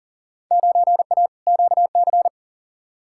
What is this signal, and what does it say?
Morse code, heard as a keyed tone, about 40 wpm, 700 Hz. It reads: 9A QC
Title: Field Day Exchanges Encore - 40wpm